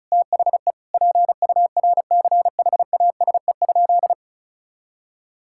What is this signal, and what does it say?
Morse code, heard as a keyed tone, about 36 wpm, 700 Hz. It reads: THE PURCHASE?
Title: Sentences from Top 1000 Words - Review Part 5 - 35wpm